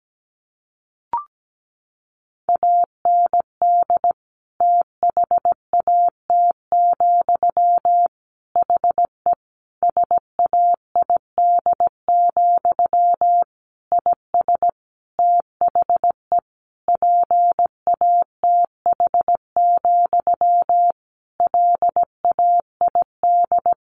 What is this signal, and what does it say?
Morse code, heard as a keyed tone, about 17 wpm, 700 Hz. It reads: AND THAT, HE SAID, IS THE PATH, LAID
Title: A Sound Of Thunder 17wpm